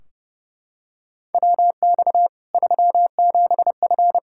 Morse code, heard as a keyed tone, about 30 wpm, 700 Hz. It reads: WX 37F
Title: Practice QSO - Exchange 2 - Info - 30wpm